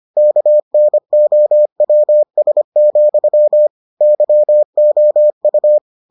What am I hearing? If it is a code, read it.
Morse code, heard as a keyed tone, about 25 wpm, 600 Hz. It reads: KNOWS, YOU